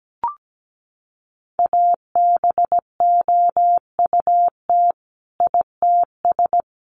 Morse code, heard as a keyed tone, about 17 wpm, 700 Hz. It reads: ABOUT ITS
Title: Sentences from Top 1200 Words 17wpm